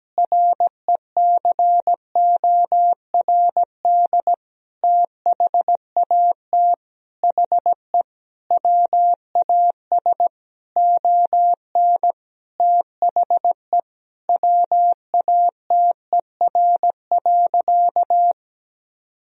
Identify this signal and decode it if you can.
Morse code, heard as a keyed tone, about 17 wpm, 700 Hz. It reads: RECORD THAT HE WAS ON THE WATER.
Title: Sentences from Top 500 Words and Abbreviations 17wpm